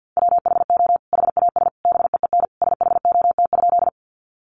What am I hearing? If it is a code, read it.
Morse code, heard as a keyed tone, about 50 wpm, 700 Hz. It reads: WHO HAS BEEN SHOT?
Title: Sentences from Top 1100 Words 50wpm